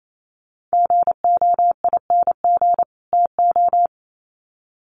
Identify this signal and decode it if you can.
Morse code, heard as a keyed tone, about 28 wpm, 700 Hz. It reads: GOING TO
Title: Sets of 2 Words 28wpm